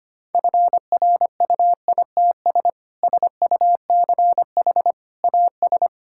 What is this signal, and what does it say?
Morse code, heard as a keyed tone, about 25 wpm, 700 Hz. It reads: FRUITS SUCH AS
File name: practice / 25wpm / Sentences with 3-Word Phrases - 25wpm